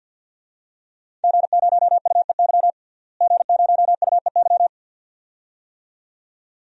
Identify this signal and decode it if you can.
Morse code, heard as a keyed tone, about 49 wpm, 700 Hz. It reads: G0UEY G0UEY
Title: DXCC-100 Call Signs - 5 Characters - Repeated 2x - 50wpm